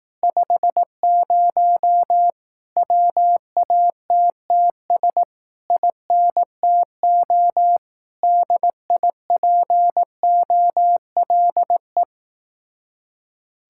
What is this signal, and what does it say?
Morse code, heard as a keyed tone, about 18 wpm, 700 Hz. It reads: 50 WATTS INTO DIPOLE
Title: Speed-Racing Common QSO Phrases Encore - 15wpm